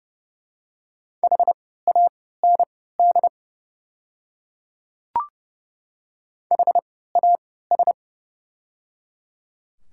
Morse code, heard as a keyed tone, about 30 wpm, spaced as for 17 wpm, 700 Hz. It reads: HAND HAS
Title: 3x Character Spacing - Top 300 Words - 30wpm